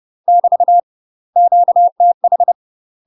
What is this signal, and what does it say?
Morse code, heard as a keyed tone, about 30 wpm, 700 Hz. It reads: <BT> QTH
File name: UK MASTER QSOs TRAINING / 30WPM / UK MASTER QSOs 0034